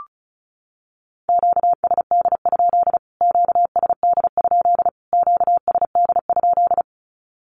Morse code, heard as a keyed tone, about 34 wpm, 700 Hz. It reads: QSD? QSD? QSD?
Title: Q-Codes Repeated 3x 35wpm